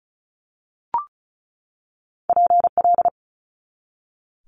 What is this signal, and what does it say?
Morse code, heard as a keyed tone, about 36 wpm, 700 Hz. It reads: PL
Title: Multi-letter Phonemes 35wpm